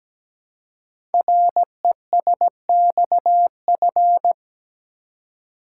Morse code, heard as a keyed tone, about 17 wpm, 700 Hz. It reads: RESXF
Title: Code Groups - 5 Letters - 17wpm